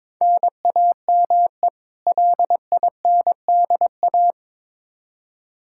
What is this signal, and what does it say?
Morse code, heard as a keyed tone, about 22 wpm, 700 Hz. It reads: NAME LINDA
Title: Practice QSO - Exchange 1 - Beginning - 22wpm